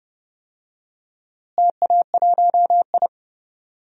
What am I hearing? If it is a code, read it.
Morse code, heard as a keyed tone, about 32 wpm, 700 Hz. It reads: TA1I
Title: DXCC-100 Call Signs - 4 Characters - Repeated 3x - 30wpm